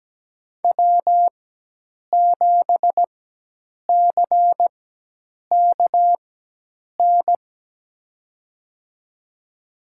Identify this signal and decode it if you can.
Morse code, heard as a keyed tone, about 17 wpm, spaced as for 8 wpm, 700 Hz. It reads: W7CKN
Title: USA Call Signs - 5 Characters - 4x Character Spacing - 17wpm